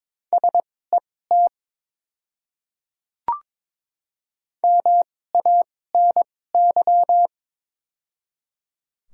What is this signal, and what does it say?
Morse code, heard as a keyed tone, about 21 wpm, spaced as for 16 wpm, 700 Hz. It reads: SET MANY
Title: 2x Character Spacing - Top 200 Words - 22wpm